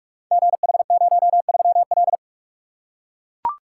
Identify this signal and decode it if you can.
Morse code, heard as a keyed tone, about 44 wpm, 690 Hz. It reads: GH03L
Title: Code Groups - 5 Letter:Numbers - 45wpm